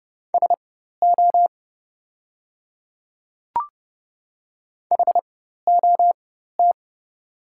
Morse code, heard as a keyed tone, about 30 wpm, spaced as for 14 wpm, 700 Hz. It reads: SO HOT
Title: 4x Character Spacing - Top 100 Words - 30wpm